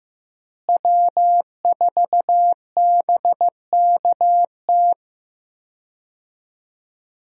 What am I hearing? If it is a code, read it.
Morse code, heard as a keyed tone, about 15 wpm, 700 Hz. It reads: W4BKT
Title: USA Call Signs - 5 Characters - Repeated 2x - 15wpm